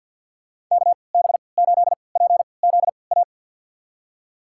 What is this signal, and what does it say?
Morse code, heard as a keyed tone, about 52 wpm, spaced as for 28 wpm, 700 Hz. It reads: KB8PZA
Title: USA Call Signs - 6 Characters - 3x Character Spacing - 50wpm